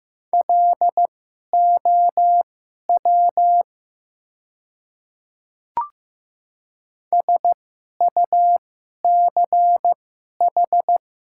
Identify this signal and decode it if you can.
Morse code, heard as a keyed tone, about 15 wpm, spaced as for 11 wpm, 700 Hz. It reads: LOW SUCH
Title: 2x Character Spacing - Top 300 Words - 15wpm